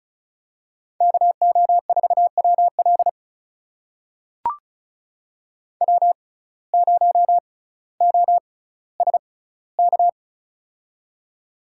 Morse code, heard as a keyed tone, about 36 wpm, spaced as for 18 wpm, 700 Hz. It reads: KO4WL W0OSK
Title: USA Call Signs - 5 Characters - 6x Character Spacing - 35wpm